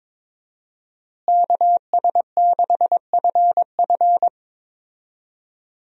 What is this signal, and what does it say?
Morse code, heard as a keyed tone, about 22 wpm, 700 Hz. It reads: KS6FF
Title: USA Call Signs - 5 Characters - Repeated 2x - 22wpm